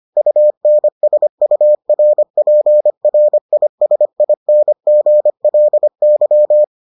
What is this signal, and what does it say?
Morse code, heard as a keyed tone, about 25 wpm, 600 Hz. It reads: UNSURPRISINGLY